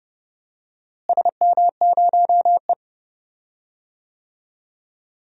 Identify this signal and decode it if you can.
Morse code, heard as a keyed tone, about 30 wpm, 700 Hz. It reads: SM0E
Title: DXCC-100 Call Signs - 4 Characters - Repeated 2x - 30wpm